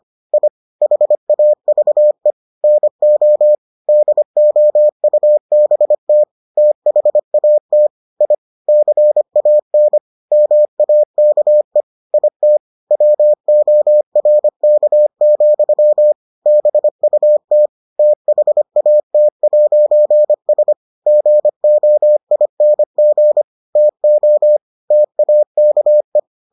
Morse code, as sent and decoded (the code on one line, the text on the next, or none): .. / .... .- ...- . / -. --- / -.. --- ..- -... - / - .... .- - / .. / -.-. .- -. / -- .- -.- . / .. - / .-- --- .-. -.- --..-- / -... ..- - / - .... .- - .----. ... / --. --- .. -. --. / - --- / - .- -.- .
I HAVE NO DOUBT THAT I CAN MAKE IT WORK, BUT THAT'S GOING TO TAKE